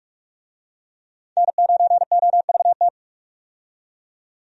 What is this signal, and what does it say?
Morse code, heard as a keyed tone, about 45 wpm, 700 Hz. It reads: N9OVT
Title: USA Call Signs - 5 Characters - 45wpm